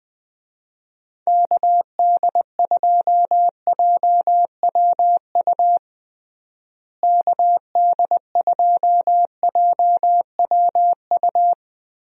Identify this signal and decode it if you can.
Morse code, heard as a keyed tone, about 20 wpm, 700 Hz. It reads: KD2JWU KD2JWU
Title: USA Call Signs - 6 Characters - Repeated 3x - 20wpm